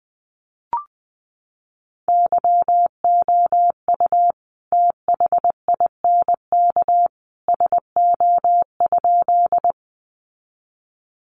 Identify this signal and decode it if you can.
Morse code, heard as a keyed tone, about 20 wpm, 700 Hz. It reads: YOU THINK SO?
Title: Sentences from Top 200 Words 20wpm